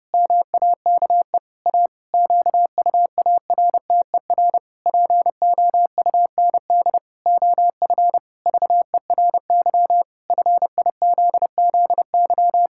Morse code, heard as a keyed tone, about 32 wpm, 700 Hz. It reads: MAKE A QUARTER POUND OF VERY FIZZY
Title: Pangram Challenge v2 - 30wpm